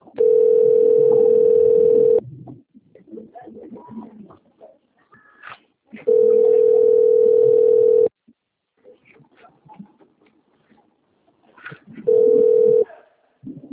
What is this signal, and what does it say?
Ringback tone (North America), 440 + 480 Hz.
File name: rg-1600-unknown- - - .8770